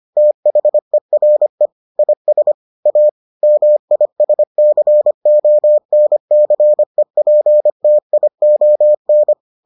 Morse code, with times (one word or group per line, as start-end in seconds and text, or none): 0.2-1.7 THERE
2.0-2.5 IS
2.9-3.1 A
3.4-9.3 MISCONCEPTION